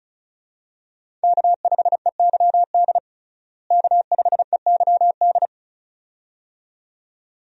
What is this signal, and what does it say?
Morse code, heard as a keyed tone, about 34 wpm, 700 Hz. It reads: K5EYD K5EYD
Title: USA Call Signs - 5 Characters - Repeated 2x - 35wpm